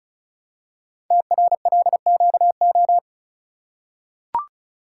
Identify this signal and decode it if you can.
Morse code, heard as a keyed tone, about 36 wpm, 700 Hz. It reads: TRLQO
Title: Code Groups - 5 Letter:Numbers - 35wpm